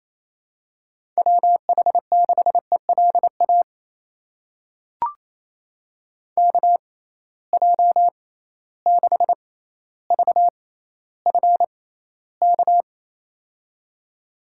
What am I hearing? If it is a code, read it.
Morse code, heard as a keyed tone, about 28 wpm, spaced as for 14 wpm, 700 Hz. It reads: WH6ELA KJ6VFK
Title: USA Call Signs - 6 Characters - 6x Character Spacing - 28wpm